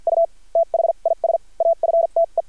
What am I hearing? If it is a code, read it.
Morse code, heard as a keyed tone, about 51 wpm, spaced as for 41 wpm, 670 Hz. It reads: U THIS AFTE
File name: archived / CRSnet